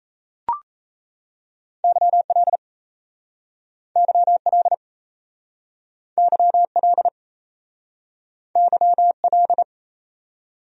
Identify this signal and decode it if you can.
Morse code, heard as a keyed tone, about 38 wpm, 700 Hz. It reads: YL YL YL YL